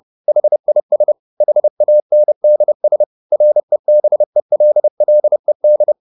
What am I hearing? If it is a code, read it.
Morse code, heard as a keyed tone, about 30 wpm, 600 Hz. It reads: HIS HANDS REBELLED